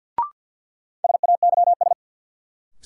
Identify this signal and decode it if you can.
Morse code, heard as a keyed tone, about 47 wpm, 700 Hz. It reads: SAYS
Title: Sets of 2 Words - New Word Drill - 50wpm